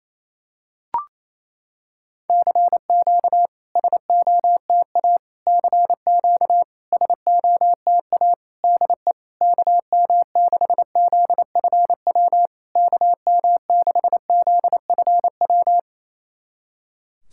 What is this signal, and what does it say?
Morse code, heard as a keyed tone, about 28 wpm, 700 Hz. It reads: CQ SOTA CQ SOTA DE KM6ZFW KM6ZFW
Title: SOTA QSO Practice 28wpm